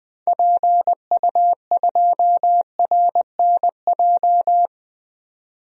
Morse code, heard as a keyed tone, about 21 wpm, 700 Hz. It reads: PU2RNJ
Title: DXCC-100 Call Signs - 6 Characters - 20wpm